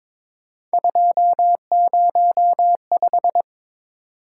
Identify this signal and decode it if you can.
Morse code, heard as a keyed tone, about 22 wpm, 700 Hz. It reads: 205
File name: practice / 22wpm / Three Letter-Number Combinations - 22wpm